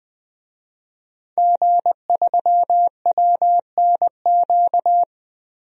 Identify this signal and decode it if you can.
Morse code, heard as a keyed tone, about 20 wpm, 700 Hz. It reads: G3WNQ
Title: DXCC-100 Call Signs - 5 Characters - Repeated 3x - 20wpm